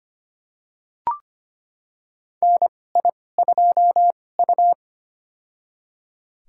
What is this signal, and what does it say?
Morse code, heard as a keyed tone, about 25 wpm, spaced as for 18 wpm, 700 Hz. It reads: NI2U
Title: USA Call Signs - 2x Character Spacing - 25wpm